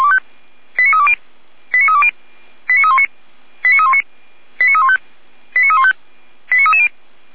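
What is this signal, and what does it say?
Five-tone selective call (ZVEI), heard as quick runs of five tones, tones of about 70 ms. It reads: 89320 89320 89319 89319 89315 89315 89330
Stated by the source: ZVEI five-tone selective call 89110, received on SDR.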